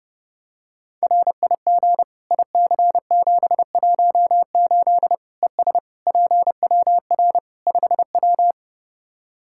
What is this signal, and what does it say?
Morse code, heard as a keyed tone, about 30 wpm, 700 Hz. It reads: RIG IC718 ES PWR 5W
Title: K1USN SST QSO Practice 30wpm